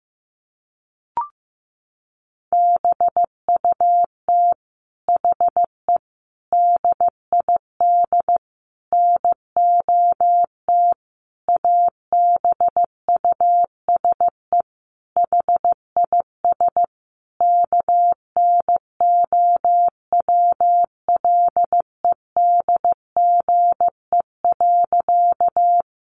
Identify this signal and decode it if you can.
Morse code, heard as a keyed tone, about 14 wpm, 700 Hz. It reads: BUT HE DID NOT ABUSE HIS KNOWLEDGE.
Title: Sentences from Top 1300 Words 15wpm